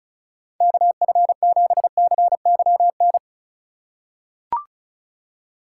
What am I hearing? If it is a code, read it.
Morse code, heard as a keyed tone, about 34 wpm, 700 Hz. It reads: KF7CYN
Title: USA Call Signs - 6 Characters - 4x Character Spacing - 35wpm